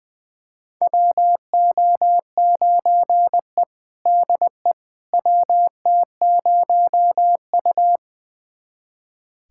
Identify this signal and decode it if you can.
Morse code, heard as a keyed tone, about 20 wpm, 700 Hz. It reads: WO9E DE WT0U
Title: USA Callsign DE Callsign - 4 Characters - 20wpm